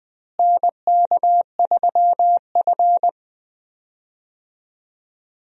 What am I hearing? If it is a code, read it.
Morse code, heard as a keyed tone, about 20 wpm, 700 Hz. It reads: NK3F
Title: USA Call Signs - 4 Characters - Repeated 3x - 20wpm